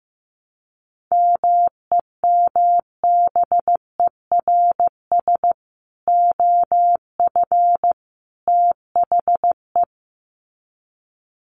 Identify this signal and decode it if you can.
Morse code, heard as a keyed tone, about 15 wpm, 700 Hz. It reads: MEMBERS OF THE
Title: Sets of 3 Words 15wpm